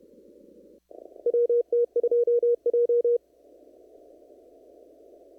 Morse code, heard as a keyed tone, about 30 wpm, 440 Hz. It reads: WT2J
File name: CW / kt5j wpx cw-op-k5pi